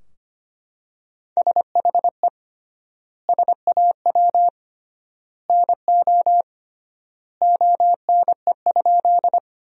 Morse code, heard as a keyed tone, about 25 wpm, 700 Hz. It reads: SHE SAW NO ONE?
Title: 3x Word-Spacing - Sentences from Top 300 Words - 25wpm